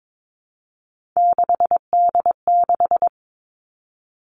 Morse code, heard as a keyed tone, about 22 wpm, 700 Hz. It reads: 6D6
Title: Three Challenging Letter-Number Combinations 22wpm